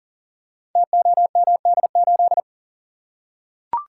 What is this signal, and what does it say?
Morse code, heard as a keyed tone, about 43 wpm, 690 Hz. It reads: TOMD8
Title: Code Groups - 5 Letter:Numbers - 40wpm